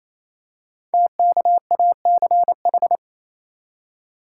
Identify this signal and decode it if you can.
Morse code, heard as a keyed tone, about 29 wpm, 700 Hz. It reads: TKACH
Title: Code Groups - 5 Letters - 28wpm